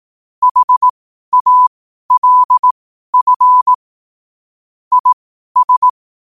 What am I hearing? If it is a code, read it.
Morse code, heard as a keyed tone, about 18 wpm, spaced as for 12 wpm, 1000 Hz. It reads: HALF IS
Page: Quotes for Thu, 14 Aug 2025 in Morse Code at 12 words per minute.